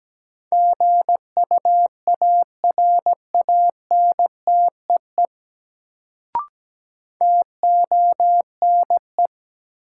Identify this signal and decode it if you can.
Morse code, heard as a keyed tone, about 17 wpm, 700 Hz. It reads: GUARANTEE TONE
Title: Sentences from Top 1600 Words - New Word Drill - 17wpm